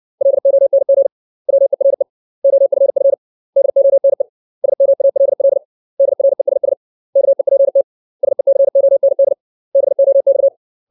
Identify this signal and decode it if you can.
Morse code, heard as a keyed tone, about 60 wpm, 550 Hz. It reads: LONG WERE OFF DONE SMALL DRESS KEPT SECOND BOX
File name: 632 words / 632 Words 60WPM